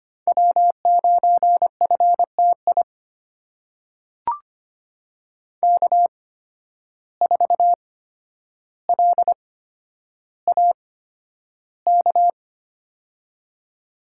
Morse code, heard as a keyed tone, about 25 wpm, spaced as for 11 wpm, 700 Hz. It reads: W9FTI K4LAK
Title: USA Call Signs - 5 Characters - 8x Character Spacing - 25wpm